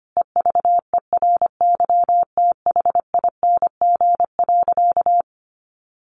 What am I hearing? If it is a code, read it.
Morse code, heard as a keyed tone, about 25 wpm, 700 Hz. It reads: EVERYTHING.
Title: Sentences from Top 700 Words 25wpm